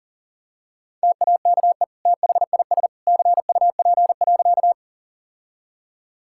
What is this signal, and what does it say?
Morse code, heard as a keyed tone, about 40 wpm, 700 Hz. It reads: TAKE THIS CUP.
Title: Sentences from Top 1600 Words 40wpm